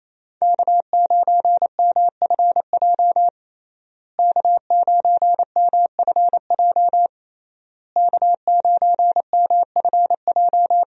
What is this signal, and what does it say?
Morse code, heard as a keyed tone, about 27 wpm, 700 Hz. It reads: K9MFJ K9MFJ K9MFJ
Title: USA Call Signs - 5 Characters - Repeated 3x - 28wpm